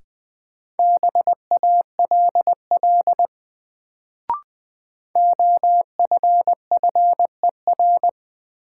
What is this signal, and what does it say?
Morse code, heard as a keyed tone, about 21 wpm, 700 Hz. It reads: BALL OFFER
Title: Sentences from Top 900 Words - New Word Drill - 20wpm